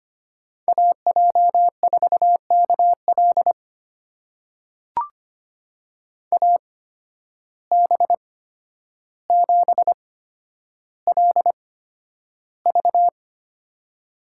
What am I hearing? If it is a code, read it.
Morse code, heard as a keyed tone, about 25 wpm, spaced as for 11 wpm, 700 Hz. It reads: AJ4KL AB7LV
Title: USA Call Signs - 5 Characters - 8x Character Spacing - 25wpm